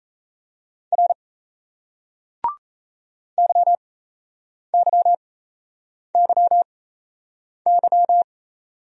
Morse code, heard as a keyed tone, about 38 wpm, spaced as for 8 wpm, 700 Hz. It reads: R YYYY